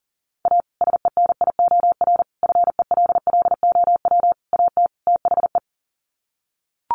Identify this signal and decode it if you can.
Morse code, heard as a keyed tone, about 40 wpm, 700 Hz. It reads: A SENIOR FELLOW AT THE
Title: Sets of 5 Words Encore 40wpm